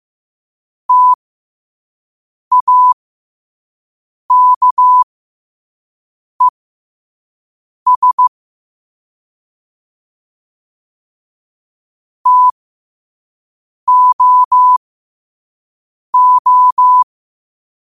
Morse code, heard as a keyed tone, about 15 wpm, spaced as for 5 wpm, 1000 Hz. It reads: TAKES TOO
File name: qotdfiles / 5wpm